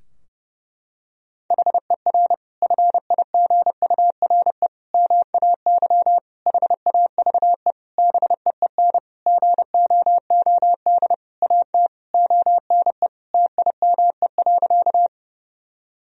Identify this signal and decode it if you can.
Morse code, heard as a keyed tone, about 30 wpm, 700 Hz. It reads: HER FIGURE MAY HAVE BEEN GOOD AT ONE TIME.
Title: Sentences from Top 1000 Words - Review Part 2 - 30wpm